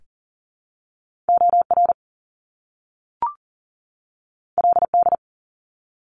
Morse code, heard as a keyed tone, about 40 wpm, 700 Hz. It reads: OR LD
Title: Top Two Letter Combinations in Words 40wpm